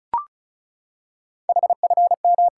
Morse code, heard as a keyed tone, about 36 wpm, 700 Hz. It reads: HFM
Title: Code Groups - 5 Letter:Numbers - 35wpm